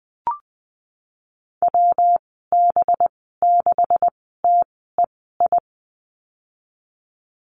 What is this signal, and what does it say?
Morse code, heard as a keyed tone, about 20 wpm, spaced as for 15 wpm, 700 Hz. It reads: WB6TEI
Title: 231 Lesson 37g - Letter x - Call Signs 20wpm